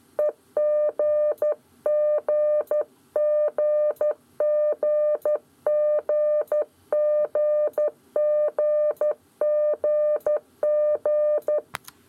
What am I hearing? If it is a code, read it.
Morse code, heard as a keyed tone, about 12 wpm, 600 Hz. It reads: EGGGGGGGGG
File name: 7 Esercizio qgzl